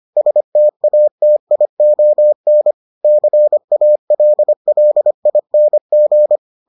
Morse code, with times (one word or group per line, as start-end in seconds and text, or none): 0.2-2.7 STATION
3.0-6.4 CALLING